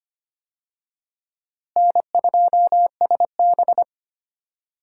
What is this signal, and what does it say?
Morse code, heard as a keyed tone, about 25 wpm, 700 Hz. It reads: N2SB
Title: USA Call Signs - 4 Characters - Repeated 2x - 25wpm